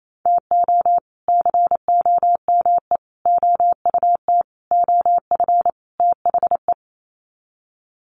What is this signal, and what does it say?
Morse code, heard as a keyed tone, about 28 wpm, 700 Hz. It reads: TO COME OUT OF THE
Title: Sets of 5 Words Encore2 28wpm